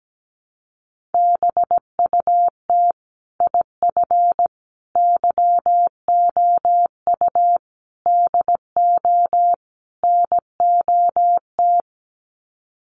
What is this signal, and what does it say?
Morse code, heard as a keyed tone, about 17 wpm, 700 Hz. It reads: BUT IF YOU DO NOT
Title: Sets of 5 Words Encore 17wpm